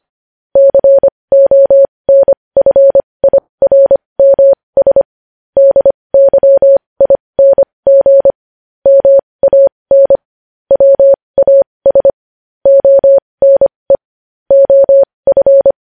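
Morse code, heard as a keyed tone, about 25 wpm, spaced as for 20 wpm, 550 Hz. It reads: CONFIRMS DYING MAN WAS ONE OF
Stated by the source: Morse code news headlines sent at 20 wpm.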